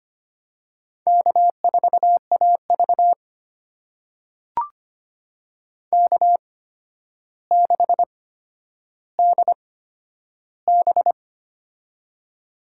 Morse code, heard as a keyed tone, about 25 wpm, spaced as for 11 wpm, 700 Hz. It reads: K4AV K6DB
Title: USA Call Signs - 4 Characters - 8x Character Spacing - 25wpm